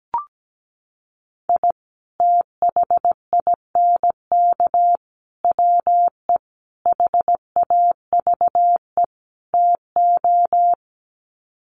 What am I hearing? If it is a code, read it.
Morse code, heard as a keyed tone, about 17 wpm, 700 Hz. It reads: I THINK WE HAVE TO
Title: Sets of 5 Words 17wpm